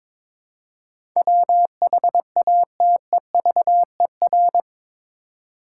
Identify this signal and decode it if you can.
Morse code, heard as a keyed tone, about 22 wpm, 700 Hz. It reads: WHATEVER
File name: practice / 22wpm / Eight Letter Words - 22wpm